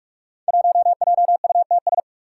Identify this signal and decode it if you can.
Morse code, heard as a keyed tone, about 44 wpm, 700 Hz. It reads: 1JUTS
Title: Five Challenging Letter-Number Combinations 45wpm